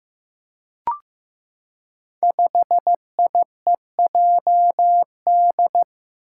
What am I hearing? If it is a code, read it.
Morse code, heard as a keyed tone, about 15 wpm, 700 Hz. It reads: 5IEJD
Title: Five Challenging Letter-Number Combinations 15wpm